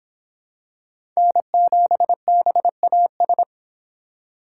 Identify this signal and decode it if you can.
Morse code, heard as a keyed tone, about 26 wpm, 700 Hz. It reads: N7BAS